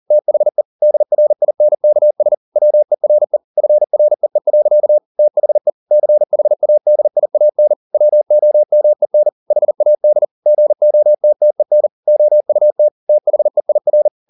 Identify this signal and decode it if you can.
Morse code, heard as a keyed tone, about 40 wpm, 600 Hz. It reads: THE DRINKS WERE FREE. THE CHADIAN WOMEN HAD GOTTEN OUT THEIR